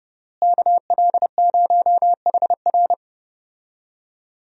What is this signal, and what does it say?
Morse code, heard as a keyed tone, about 30 wpm, 700 Hz. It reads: KL0HR
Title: USA Call Signs - 5 Characters - Repeated 3x - 30wpm